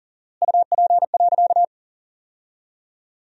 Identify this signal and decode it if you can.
Morse code, heard as a keyed tone, about 37 wpm, 700 Hz. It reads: UP.
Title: 4x Word-Spacing - Sentences from Top 300 Words - 40wpm